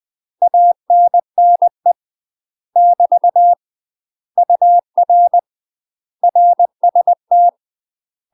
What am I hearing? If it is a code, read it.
Morse code, heard as a keyed tone, about 20 wpm, 700 Hz. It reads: ANNE <BT> UR RST
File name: UK MASTER QSOs TRAINING / 20WPM / UK MASTER QSOs 0060